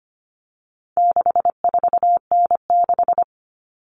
Morse code, heard as a keyed tone, about 25 wpm, 700 Hz. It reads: 64N6
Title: All Cut Numbers - 4-Digit 25wpm